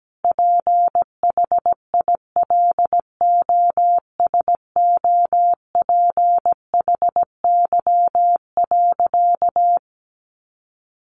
Morse code, heard as a keyed tone, about 17 wpm, 700 Hz. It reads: PHILOSOPHY.
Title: Sentences from Top 900 Words 17wpm